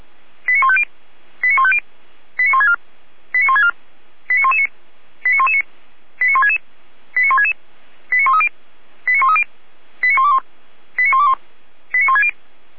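Five-tone selective call (ZVEI), heard as quick runs of five tones, tones of about 70 ms. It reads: 89150 89150 89164 89164 89119 89119 89166 89166 89130 89130 89121 89121 89160
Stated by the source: ZVEI five-tone selective call received on SDR.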